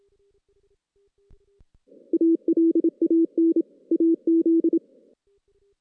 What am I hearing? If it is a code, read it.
Morse code, heard as a keyed tone, about 27 wpm, 330 Hz. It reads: ALAN AZ
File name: 2016 JAN CW NAQP / 2016 Jan naqp k5ot - 0-6